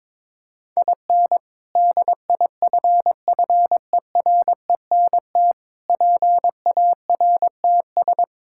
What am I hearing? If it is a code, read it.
Morse code, heard as a keyed tone, about 22 wpm, 700 Hz. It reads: IN DIFFERENT PARTS
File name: practice / 22wpm / Sentences with 3-Word Phrases - 22wpm